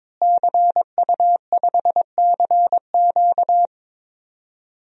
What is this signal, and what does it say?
Morse code, heard as a keyed tone, about 22 wpm, 700 Hz. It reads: CU5CQ
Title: DXCC-100 Call Signs - 5 Characters - Repeated 3x - 22wpm